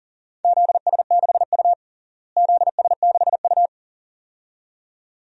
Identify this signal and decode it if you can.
Morse code, heard as a keyed tone, about 40 wpm, 700 Hz. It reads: ZS6U ZS6U
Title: DXCC-100 Call Signs - 4 Characters - Repeated 2x - 40wpm